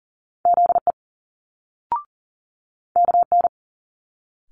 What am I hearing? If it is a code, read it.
Morse code, heard as a keyed tone, about 41 wpm, 700 Hz. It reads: ZE KN
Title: Multi-letter Phonemes Quick Fire 40wpm